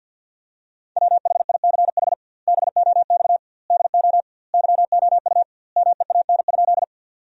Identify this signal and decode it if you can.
Morse code, heard as a keyed tone, about 50 wpm, 700 Hz. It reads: WHICH BOX DO YOU MEAN?
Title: 2x Word-Spacing - Sentences from Top 500 Words - 50wpm